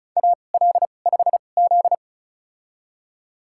Morse code, heard as a keyed tone, about 34 wpm, spaced as for 25 wpm, 700 Hz. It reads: AL5Z
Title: USA Call Signs - 4 Characters - 2x Character Spacing - 35wpm